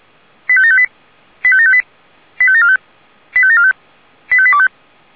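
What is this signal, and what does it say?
Five-tone selective call (ZVEI), heard as quick runs of five tones, tones of about 70 ms. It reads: 85758 85758 85745 85745 85825
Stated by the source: ZVEI five-tone call 85827 received by SDR.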